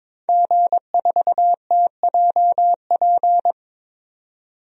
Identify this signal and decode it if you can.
Morse code, heard as a keyed tone, about 21 wpm, 700 Hz. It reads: G4TJP
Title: DXCC-100 Call Signs - 5 Characters - 22wpm